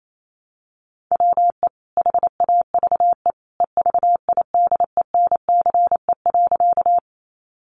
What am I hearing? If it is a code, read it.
Morse code, heard as a keyed tone, about 28 wpm, 700 Hz. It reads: WE HAVE EVIDENCE.
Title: Sentences from Top 1300 Words 28wpm